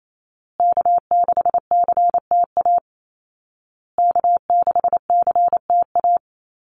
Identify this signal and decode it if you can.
Morse code, heard as a keyed tone, about 28 wpm, 700 Hz. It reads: K6CTA K6CTA
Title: Call Signs Repeated 2x - 28wpm